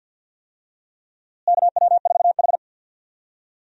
Morse code, heard as a keyed tone, about 47 wpm, 700 Hz. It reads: KW4H
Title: USA Call Signs - 4 Characters - Repeated 3x - 50wpm